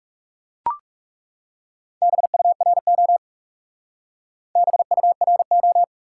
Morse code, heard as a keyed tone, about 37 wpm, 700 Hz. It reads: BURO BURO
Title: Speed-Racing Top 180 QSO Elements 30wpm